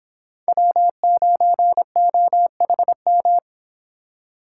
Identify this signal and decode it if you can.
Morse code, heard as a keyed tone, about 26 wpm, 700 Hz. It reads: W9OHM